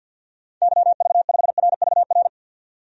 Morse code, heard as a keyed tone, about 51 wpm, 700 Hz. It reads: YV5RVR
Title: DXCC-100 Call Signs - 6 Characters - 50wpm